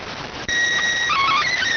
SSTV (slow-scan television), PD 290.